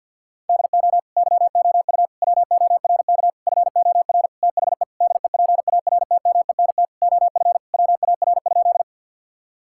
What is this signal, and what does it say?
Morse code, heard as a keyed tone, about 50 wpm, 700 Hz. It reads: DO YOU WORK FOR THE DEPARTMENT OF WAR?
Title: Sentences from Top 1000 Words - Review Part 5 - 50wpm